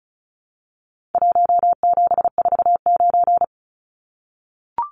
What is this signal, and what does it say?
Morse code, heard as a keyed tone, about 34 wpm, 700 Hz. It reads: 1749
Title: Numbers - Contest Style - 4-Digit 35wpm